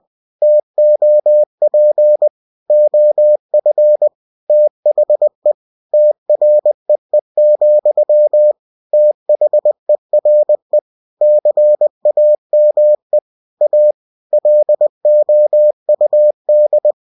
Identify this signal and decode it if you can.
Morse code, heard as a keyed tone, about 20 wpm, 600 Hz. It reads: TOP OF THE TREE, THERE CAME A LOUD
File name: cw-books / winnie-the-pooh / winnie-20wpm-page-0002